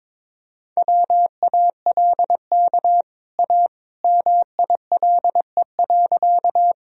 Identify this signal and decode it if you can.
Morse code, heard as a keyed tone, about 22 wpm, 700 Hz. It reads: WALK A MILE.
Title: Sentences from Top 1000 Words - Review Part 1 - 22wpm